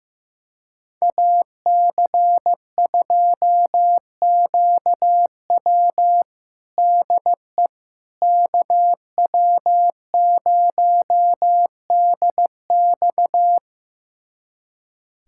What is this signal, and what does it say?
Morse code, heard as a keyed tone, about 15 wpm, 700 Hz. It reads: AC2QW DE KW0DX
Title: USA Callsign DE Callsign - 5 Characters - 15wpm